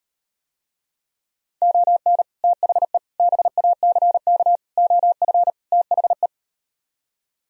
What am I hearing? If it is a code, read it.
Morse code, heard as a keyed tone, about 38 wpm, 700 Hz. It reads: ON THE BACK OF THE